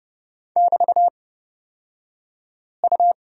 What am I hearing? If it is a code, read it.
Morse code, heard as a keyed tone, about 29 wpm, spaced as for 5 wpm, 690 Hz. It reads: <BT>U
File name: Lesson 36d - BT - Two Characters 30wpm